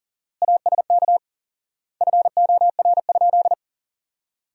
Morse code, heard as a keyed tone, about 37 wpm, 700 Hz. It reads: ASK FOR?
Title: 4x Word-Spacing - Sentences from Top 200 Words - 40wpm